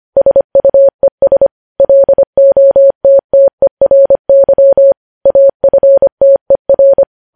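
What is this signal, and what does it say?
Morse code, heard as a keyed tone, about 25 wpm, 550 Hz. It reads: SUES LOTTERY AFTER
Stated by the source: Morse code news headlines sent at 25 wpm.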